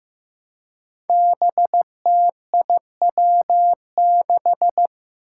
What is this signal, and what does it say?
Morse code, heard as a keyed tone, about 15 wpm, 700 Hz. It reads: BTIW6
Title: Five Challenging Letter-Number Combinations 15wpm